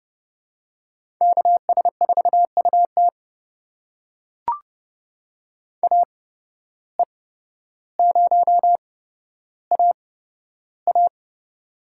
Morse code, heard as a keyed tone, about 32 wpm, spaced as for 13 wpm, 700 Hz. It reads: KS4UT AE0AA
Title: USA Call Signs - 5 Characters - 8x Character Spacing - 30wpm